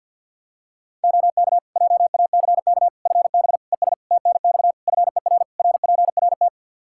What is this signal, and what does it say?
Morse code, heard as a keyed tone, about 50 wpm, 700 Hz. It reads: OK JACK FB ES TNX FER RPRT
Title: Practice QSO - Exchange 2 - Info - 50wpm